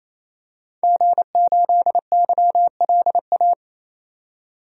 Morse code, heard as a keyed tone, about 28 wpm, 700 Hz. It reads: G8YLA
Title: DXCC-100 Call Signs - 5 Characters - 28wpm